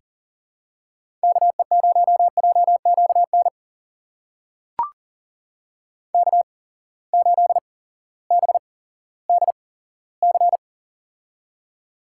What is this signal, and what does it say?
Morse code, heard as a keyed tone, about 42 wpm, spaced as for 19 wpm, 700 Hz. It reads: KE0JQN K8BDC
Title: USA Call Signs - 8x Character Spacing - 40wpm